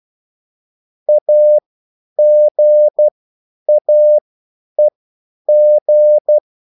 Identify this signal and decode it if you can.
Morse code, heard as a keyed tone, about 12 wpm, spaced as for 9 wpm, 600 Hz. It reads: AGAEG